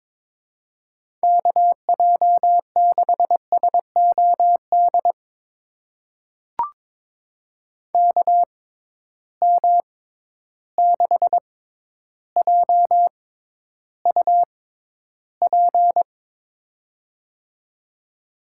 Morse code, heard as a keyed tone, about 22 wpm, spaced as for 11 wpm, 700 Hz. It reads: KJ6SOD KM6JUP
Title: USA Call Signs - 6 Characters - 6x Character Spacing - 22wpm